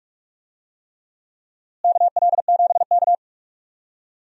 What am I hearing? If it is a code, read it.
Morse code, heard as a keyed tone, about 41 wpm, 700 Hz. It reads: KL7K
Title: DXCC-100 Call Signs - 4 Characters - Repeated 3x - 45wpm